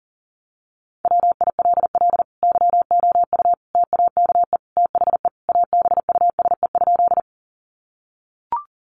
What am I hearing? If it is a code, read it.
Morse code, heard as a keyed tone, about 40 wpm, 700 Hz. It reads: WILL YOU TAKE THE ABUSE?
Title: Sentences from Top 1300 Words 40wpm